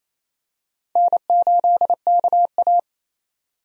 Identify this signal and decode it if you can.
Morse code, heard as a keyed tone, about 28 wpm, 700 Hz. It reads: N8KA
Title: USA Call Signs - 4 Characters - 28wpm